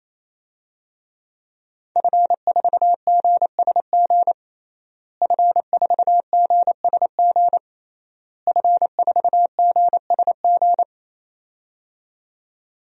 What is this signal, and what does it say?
Morse code, heard as a keyed tone, about 28 wpm, 690 Hz. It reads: F4GSG F4GSG F4GSG
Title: DXCC-100 Call Signs - 5 Characters - Repeated 3x - 28wpm